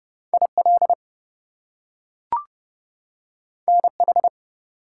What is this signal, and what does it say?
Morse code, heard as a keyed tone, about 30 wpm, 700 Hz. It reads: IL NH
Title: US States and Canadian Provinces 30wpm